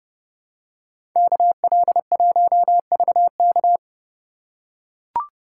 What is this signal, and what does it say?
Morse code, heard as a keyed tone, about 29 wpm, 700 Hz. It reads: KL1VK